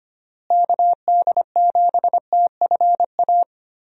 Morse code, heard as a keyed tone, about 25 wpm, 700 Hz. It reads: KD7TFA
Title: USA Call Signs - 6 Characters - Repeated 3x - 25wpm